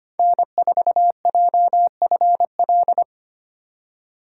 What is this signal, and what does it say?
Morse code, heard as a keyed tone, about 25 wpm, 700 Hz. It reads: N4JFL
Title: USA Call Signs - 5 Characters - 4x Character Spacing - 25wpm